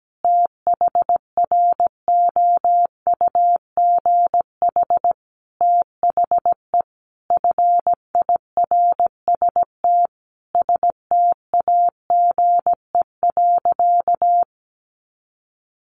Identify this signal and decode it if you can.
Morse code, heard as a keyed tone, about 17 wpm, 700 Hz. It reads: THROUGH THE FIRST STAGE.
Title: Sentences from Top 1200 Words 17wpm